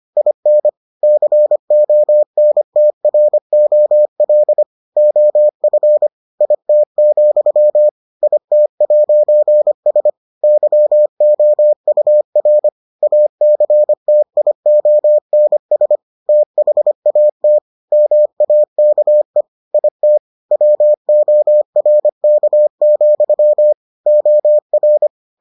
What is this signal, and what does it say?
Morse code, heard as a keyed tone, about 25 wpm, 600 Hz. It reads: IN CONTROL OF IT, IT'S YOUR ACTIONS THAT MAKE IT WORK, OR